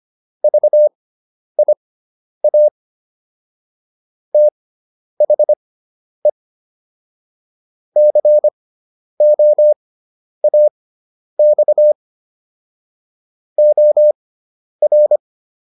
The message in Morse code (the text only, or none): VIA THE COAX OR